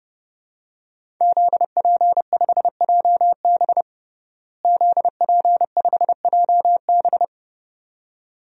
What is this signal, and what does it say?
Morse code, heard as a keyed tone, about 30 wpm, 700 Hz. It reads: ZP5JB ZP5JB
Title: DXCC-100 Call Signs - 5 Characters - Repeated 2x - 30wpm